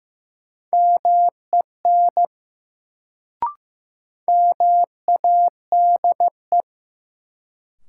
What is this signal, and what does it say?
Morse code, heard as a keyed tone, about 15 wpm, 700 Hz. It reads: MEN MADE
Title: Sentences from Top 200 Words - New Word Drill - 15wpm